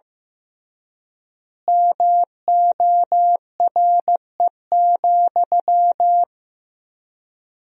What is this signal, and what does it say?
Morse code, heard as a keyed tone, about 15 wpm, 700 Hz. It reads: MORE,
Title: 3x Word-Spacing - Sentences from Top 300 Words - 15wpm